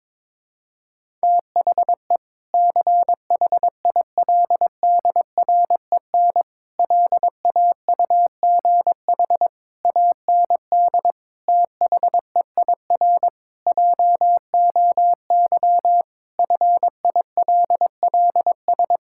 Morse code, heard as a keyed tone, about 21 wpm, 700 Hz. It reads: THE CHILDREN LAUGH AND THEIR JOY FILLS
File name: practice / 22wpm / Sentences with 2-Word Phrases - 22wpm